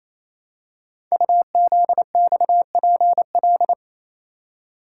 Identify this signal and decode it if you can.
Morse code, heard as a keyed tone, about 29 wpm, 700 Hz. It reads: UZXPL
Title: Code Groups - 5 Letters - 28wpm